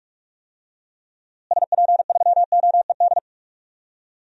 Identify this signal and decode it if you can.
Morse code, heard as a keyed tone, about 43 wpm, 700 Hz. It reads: SP3OED
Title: DXCC-100 Call Signs - 6 Characters - Repeated 3x - 45wpm